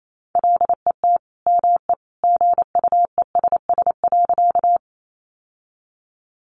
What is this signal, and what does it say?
Morse code, heard as a keyed tone, about 28 wpm, 700 Hz. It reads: LET ME GUESS.
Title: Sentences from Top 1500 Words 28wpm